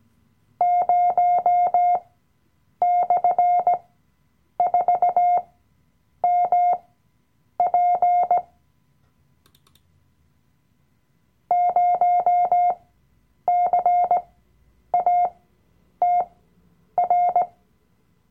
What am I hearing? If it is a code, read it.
Morse code, heard as a keyed tone, about 18 wpm, spaced as for 8 wpm, 710 Hz. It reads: 0/4MP 0CATR
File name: Workshop Dateien / CW-Online-Kurs / Lektion 007